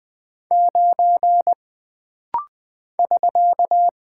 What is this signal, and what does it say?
Morse code, heard as a keyed tone, about 20 wpm, 700 Hz. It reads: 9 <SK>
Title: Single Character (Letter:Number:Punctuation:Prosign) - Rapid-Fire - 20wpm